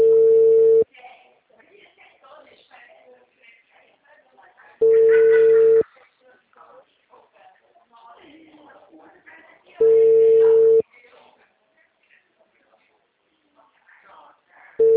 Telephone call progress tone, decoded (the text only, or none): ringback tone